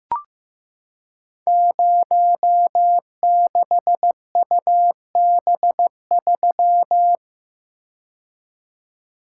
Morse code, heard as a keyed tone, about 15 wpm, 700 Hz. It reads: 06UB3
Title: Code Groups - 5 Letter:Numbers - 15wpm